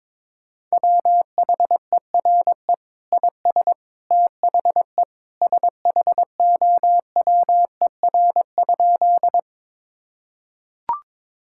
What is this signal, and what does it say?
Morse code, heard as a keyed tone, about 22 wpm, 700 Hz. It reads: WHERE IS THE SHOWER?
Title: Sentences from Top 1600 Words 22wpm